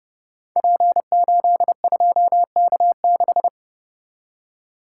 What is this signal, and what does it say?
Morse code, heard as a keyed tone, about 30 wpm, 700 Hz. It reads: P82K6
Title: Code Groups - 5 Letter:Numbers - 30wpm